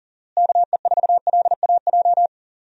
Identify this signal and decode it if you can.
Morse code, heard as a keyed tone, about 40 wpm, 700 Hz. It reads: KE4LAJ
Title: USA Call Signs - 6 Characters - Repeated 3x - 40wpm